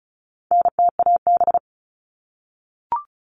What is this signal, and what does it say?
Morse code, heard as a keyed tone, about 34 wpm, 700 Hz. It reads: NTAB
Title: All Cut Numbers - 4-Digit 35wpm